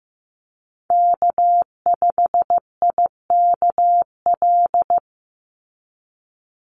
Morse code, heard as a keyed tone, about 15 wpm, 700 Hz. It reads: K5IKL
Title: USA Call Signs 15wpm